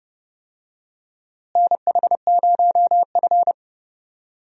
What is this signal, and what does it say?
Morse code, heard as a keyed tone, about 32 wpm, 700 Hz. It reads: NH0F
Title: USA Call Signs - 4 Characters - Repeated 2x - 30wpm